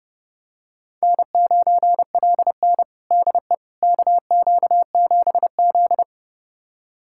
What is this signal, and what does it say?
Morse code, heard as a keyed tone, about 30 wpm, 700 Hz. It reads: N9LN DE KQ7Z
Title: USA Callsign DE Callsign - 4 Characters - 30wpm